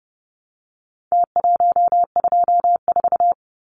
Morse code, heard as a keyed tone, about 30 wpm, 700 Hz. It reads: T124
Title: Numbers - Contest Style - 4-Digit 30wpm